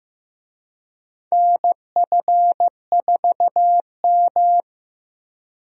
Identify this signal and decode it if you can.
Morse code, heard as a keyed tone, about 15 wpm, 700 Hz. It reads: NF4M
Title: USA Call Signs - 4 Characters - Repeated 2x - 15wpm